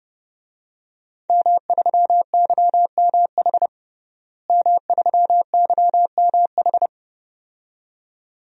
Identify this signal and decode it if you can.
Morse code, heard as a keyed tone, about 30 wpm, 700 Hz. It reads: M3YMH M3YMH
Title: DXCC-100 Call Signs - 5 Characters - Repeated 2x - 30wpm